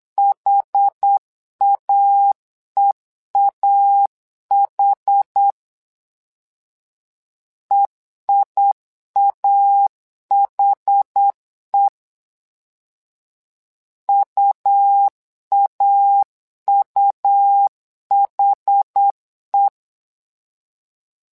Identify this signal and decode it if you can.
Morse code, heard as a keyed tone, about 8 wpm, 800 Hz. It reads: HAEAH EIAHE UAUHE
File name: LEZIONI / 2 UHDN / 2 UH 8wpm 2129